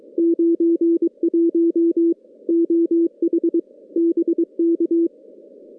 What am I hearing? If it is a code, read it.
Morse code, heard as a keyed tone, about 22 wpm, 340 Hz. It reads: 91 OH BK
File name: contests / CW-SS-k5tr- -2100